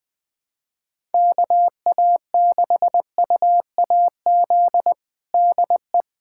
Morse code, heard as a keyed tone, about 20 wpm, 700 Hz. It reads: KA6UAZ DE
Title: USA Callsign DE Callsign - 6 Characters - 20wpm